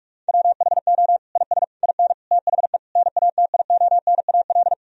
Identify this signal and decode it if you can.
Morse code, heard as a keyed tone, about 41 wpm, 700 Hz. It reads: WHO IS IN THE NATIONAL
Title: Sentences from Top 1000 Words - Review Part 4 - 45wpm